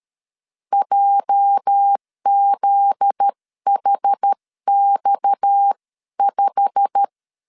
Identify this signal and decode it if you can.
Morse code, heard as a keyed tone, about 13 wpm, 800 Hz. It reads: JZHX5
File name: CW / DET / 1 9 1500C S A 12wpm 3800